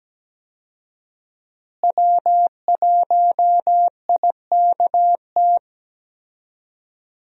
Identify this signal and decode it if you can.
Morse code, heard as a keyed tone, about 17 wpm, 700 Hz. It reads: W1IKT
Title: USA Call Signs - 5 Characters - Repeated 2x - 17wpm